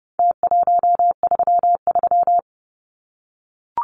Morse code, heard as a keyed tone, about 30 wpm, 700 Hz. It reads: T133
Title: Numbers - Contest Style - 4-Digit 30wpm